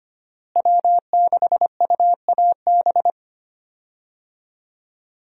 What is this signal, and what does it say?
Morse code, heard as a keyed tone, about 25 wpm, 700 Hz. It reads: W6UAB
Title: USA Call Signs - 5 Characters - Repeated 2x - 25wpm